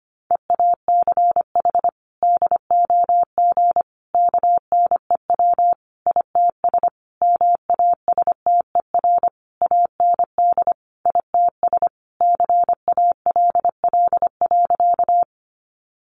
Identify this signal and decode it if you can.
Morse code, heard as a keyed tone, about 25 wpm, 700 Hz. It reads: EACH DOG KNEW ITS MASTER AND ITS CALL.
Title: Sentences from Top 1300 Words 25wpm